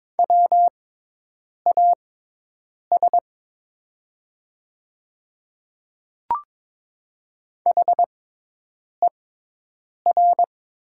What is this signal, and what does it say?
Morse code, heard as a keyed tone, about 22 wpm, spaced as for 8 wpm, 700 Hz. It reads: WAS HER
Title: 6x Character Spacing - Top 100 Words - 22wpm